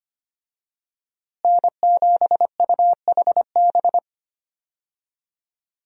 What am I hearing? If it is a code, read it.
Morse code, heard as a keyed tone, about 25 wpm, 700 Hz. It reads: N7UHB
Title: USA Call Signs - 5 Characters - Repeated 3x - 25wpm